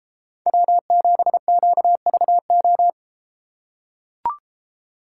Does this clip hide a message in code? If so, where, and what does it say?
Yes: Morse code from 0.5 to 2.9 s, heard as a keyed tone, about 33 wpm, 700 Hz. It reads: W7QVO